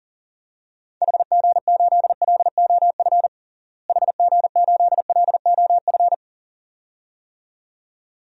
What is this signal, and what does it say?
Morse code, heard as a keyed tone, about 40 wpm, 700 Hz. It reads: HG8LOF HG8LOF
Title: DXCC-100 Call Signs - 6 Characters - Repeated 2x - 40wpm